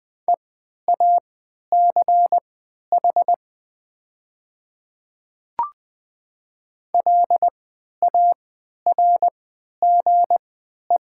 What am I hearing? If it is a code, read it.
Morse code, heard as a keyed tone, about 20 wpm, spaced as for 11 wpm, 700 Hz. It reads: EACH LARGE
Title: 3x Character Spacing - Top 200 Words - 20wpm